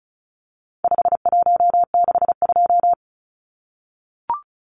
Morse code, heard as a keyed tone, about 34 wpm, 700 Hz. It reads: 5162
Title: Numbers - 4-Digit 35wpm